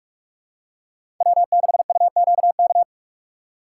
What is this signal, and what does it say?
Morse code, heard as a keyed tone, about 47 wpm, 700 Hz. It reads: W6UQK
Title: USA Call Signs - 5 Characters - 6x Character Spacing - 45wpm